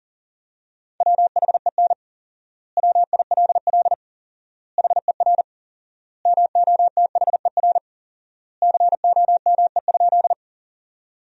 Morse code, heard as a keyed tone, about 40 wpm, 700 Hz. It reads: WHEN WILL HER MOTHER COME?
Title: 4x Word-Spacing - Sentences from Top 200 Words - 40wpm